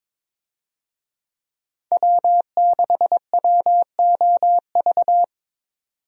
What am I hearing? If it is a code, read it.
Morse code, heard as a keyed tone, about 22 wpm, 700 Hz. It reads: W6WOV
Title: USA Call Signs - 5 Characters - 22wpm